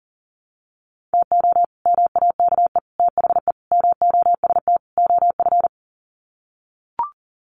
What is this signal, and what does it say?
Morse code, heard as a keyed tone, about 40 wpm, 700 Hz. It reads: TO MAKE THE MOST OF
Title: Sets of 5 Words Encore2 40wpm